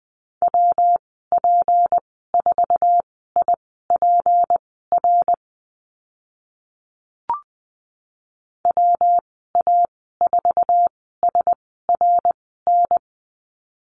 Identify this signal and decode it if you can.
Morse code, heard as a keyed tone, about 20 wpm, spaced as for 14 wpm, 700 Hz. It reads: WP4IPR WA4SRN
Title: 219 Lesson 35g - Number 0 - Call Signs 20wpm